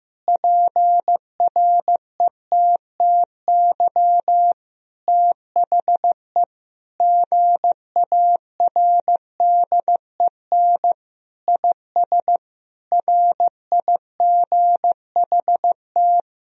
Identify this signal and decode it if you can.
Morse code, heard as a keyed tone, about 15 wpm, 700 Hz. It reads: PRETTY THE GARDEN IS RIGHT
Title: Sentences from Top 1000 Words - Review Part 4 - 15wpm